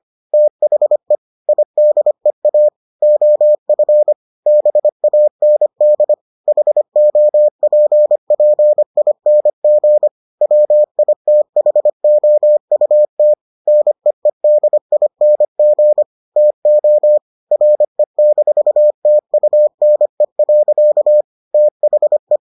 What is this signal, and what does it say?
Morse code, heard as a keyed tone, about 25 wpm, 600 Hz. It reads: THE IDEA OF BAND HOPPING WITHOUT NEEDING TO RE-TUNE. THE